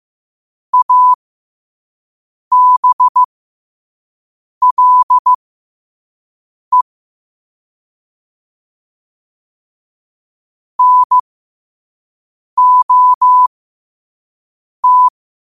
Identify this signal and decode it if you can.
Morse code, heard as a keyed tone, about 15 wpm, spaced as for 5 wpm, 1000 Hz. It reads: ABLE NOT
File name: qotdfiles / 5wpm